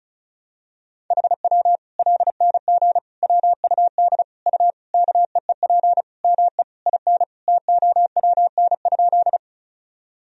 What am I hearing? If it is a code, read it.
Morse code, heard as a keyed tone, about 34 wpm, 700 Hz. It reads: HW LNG WUD U KEEP ME IN TOWN?
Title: Sentences from Top 500 Words and Abbreviations 35wpm